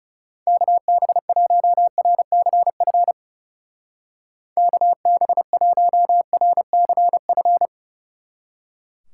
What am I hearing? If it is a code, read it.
Morse code, heard as a keyed tone, about 38 wpm, 700 Hz. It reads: KB1RCF KB1RCF